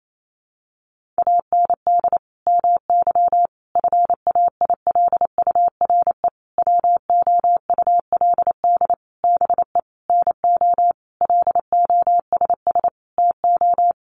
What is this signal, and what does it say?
Morse code, heard as a keyed tone, about 28 wpm, 700 Hz. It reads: AND MY FAILURE WOULD BE NO LOSS TO
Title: Sentences from Top 1400 Words 28wpm